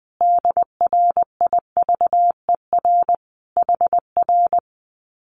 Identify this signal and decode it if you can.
Morse code, heard as a keyed tone, about 20 wpm, 700 Hz. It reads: DRIVER HR
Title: Common QSO Phrases Encore 20wpm